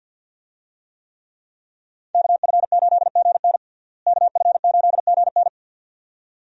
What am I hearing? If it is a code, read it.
Morse code, heard as a keyed tone, about 51 wpm, 700 Hz. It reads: KF8GN KF8GN
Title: USA Call Signs - 5 Characters - Repeated 2x - 50wpm